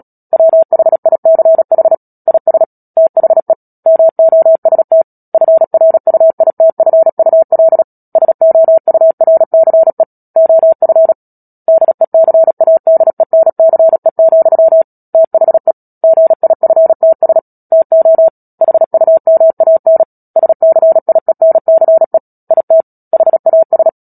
Morse code, heard as a keyed tone, about 37 wpm, 650 Hz. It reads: WHICH IS THE MOST FRUITFUL SOURCE OF DECADENCE, THE GIFTS TO HUMAN SCIENCE IT HAS
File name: Book Two / 10-Chapter 10 - Epilogue - 36 WPM